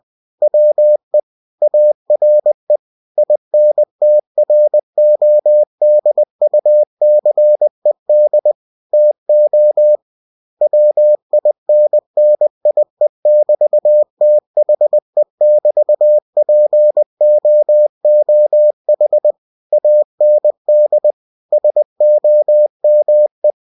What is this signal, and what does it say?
Morse code, heard as a keyed tone, about 20 wpm, 600 Hz. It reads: WE ARE INTRODUCED TO WINNIE<BT>THE<BT>POOH AND SOME
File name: cw-books / winnie-the-pooh / winnie-20wpm-page-0000